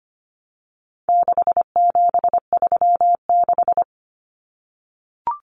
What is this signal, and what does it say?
Morse code, heard as a keyed tone, about 25 wpm, 700 Hz. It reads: 6736
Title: Common Cut Numbers - 4-Digit 25wpm